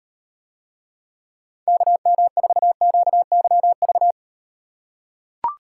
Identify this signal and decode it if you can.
Morse code, heard as a keyed tone, about 38 wpm, 700 Hz. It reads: KM4QYV